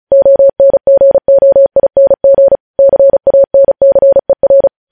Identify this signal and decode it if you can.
Morse code, heard as a keyed tone, about 36 wpm, 550 Hz. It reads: ONGOING CANCER
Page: News headlines in Morse code at 35 wpm